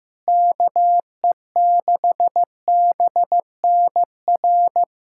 Morse code, heard as a keyed tone, about 16 wpm, 700 Hz. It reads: KE6BNR
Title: USA Callsign DE Callsign - 6 Characters - 15wpm